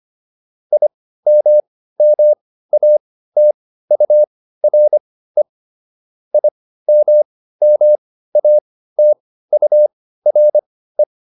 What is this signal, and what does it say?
Morse code, heard as a keyed tone, about 25 wpm, spaced as for 15 wpm, 600 Hz. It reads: IMMATURE IMMATURE